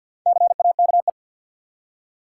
Morse code, heard as a keyed tone, about 47 wpm, 700 Hz. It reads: CAKE
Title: Four Letter Words - 50wpm